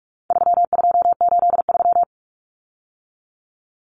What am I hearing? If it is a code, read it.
Morse code, heard as a keyed tone, about 44 wpm, 700 Hz. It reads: 3283
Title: Common Cut Numbers - 4-Digit 45wpm